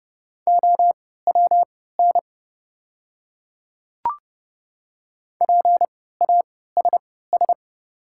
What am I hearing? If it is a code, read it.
Morse code, heard as a keyed tone, about 30 wpm, spaced as for 17 wpm, 700 Hz. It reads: OWN PASS
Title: 3x Character Spacing - Top 500 Words - 30wpm